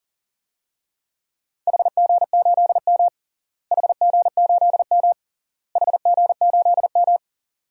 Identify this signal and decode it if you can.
Morse code, heard as a keyed tone, about 40 wpm, 700 Hz. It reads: HG8M HG8M HG8M
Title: DXCC-100 Call Signs - 4 Characters - Repeated 3x - 40wpm